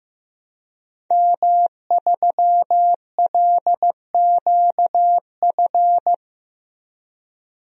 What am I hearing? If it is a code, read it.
Morse code, heard as a keyed tone, about 15 wpm, 700 Hz. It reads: M3LQF
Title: DXCC-100 Call Signs - 5 Characters - Repeated 2x - 15wpm